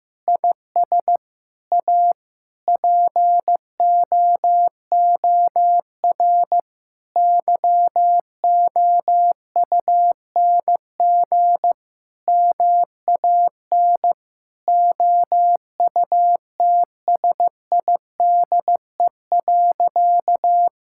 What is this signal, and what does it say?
Morse code, heard as a keyed tone, about 15 wpm, 700 Hz. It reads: IS A POOR YOUNG MAN OUTSIDE.
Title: Sentences from Top 1000 Words - Review Part 6 - 15wpm